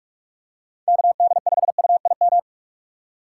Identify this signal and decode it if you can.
Morse code, heard as a keyed tone, about 46 wpm, 700 Hz. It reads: KD5UIM
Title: USA Call Signs - 6 Characters - 2x Character Spacing - 45wpm